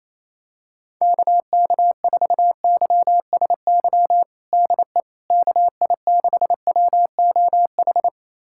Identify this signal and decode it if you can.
Morse code, heard as a keyed tone, about 27 wpm, 700 Hz. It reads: KK4YSY DE KI6WOH
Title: USA Callsign DE Callsign 28wpm